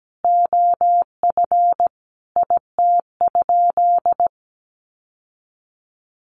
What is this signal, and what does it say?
Morse code, heard as a keyed tone, about 17 wpm, 700 Hz. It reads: OF IT?
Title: Sentences from Top 1400 Words 17wpm